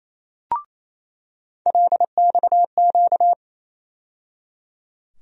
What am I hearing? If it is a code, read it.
Morse code, heard as a keyed tone, about 28 wpm, 700 Hz. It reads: LXQ
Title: USA Call Sign Postfixes - 3 Letters 28wpm